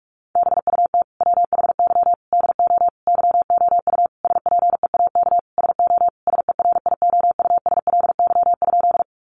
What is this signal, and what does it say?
Morse code, heard as a keyed tone, about 46 wpm, 700 Hz. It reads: BUT WHY DO YOU SPEAK SO SERIOUSLY?
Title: Sentences from Top 1200 Words 45wpm